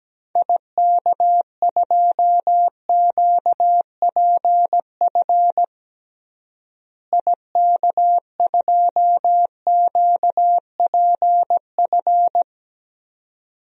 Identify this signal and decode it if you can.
Morse code, heard as a keyed tone, about 17 wpm, 700 Hz. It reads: IK2QPF IK2QPF
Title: DXCC-100 Call Signs - 6 Characters - Repeated 2x - 17wpm